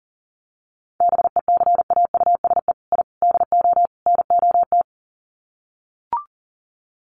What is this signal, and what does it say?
Morse code, heard as a keyed tone, about 40 wpm, 700 Hz. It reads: BECAUSE I DO NOT
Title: Sets of 4 Words 40wpm